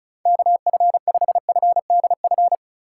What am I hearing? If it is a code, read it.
Morse code, heard as a keyed tone, about 35 wpm, 700 Hz. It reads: KF5FDF
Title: USA Call Signs - 6 Characters - 3x Character Spacing - 35wpm